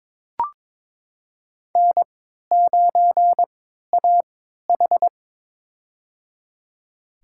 Morse code, heard as a keyed tone, about 22 wpm, spaced as for 13 wpm, 700 Hz. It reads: N9AH
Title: USA Call Signs - 4 Characters - 3x Character Spacing - 22wpm